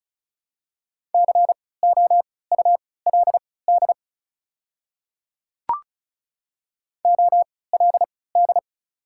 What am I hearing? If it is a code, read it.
Morse code, heard as a keyed tone, about 34 wpm, spaced as for 20 wpm, 700 Hz. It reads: COULD OLD
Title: 3x Character Spacing - Top 200 Words - 35wpm